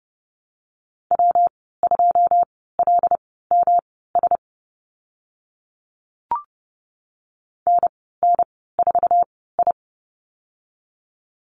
Morse code, heard as a keyed tone, about 30 wpm, spaced as for 17 wpm, 700 Hz. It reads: W2LMS NN4I
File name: Lesson 19g - Letter w - Call Signs 30wpm